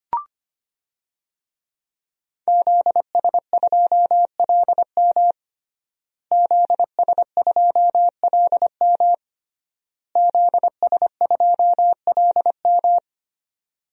Morse code, heard as a keyed tone, about 25 wpm, 700 Hz. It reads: ZS2LM ZS2LM ZS2LM
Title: DXCC-100 Call Signs - 5 Characters - Repeated 3x - 25wpm